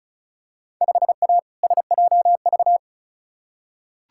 Morse code, heard as a keyed tone, about 36 wpm, 700 Hz. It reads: 5A SJV
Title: Field Day Exchanges - 35wpm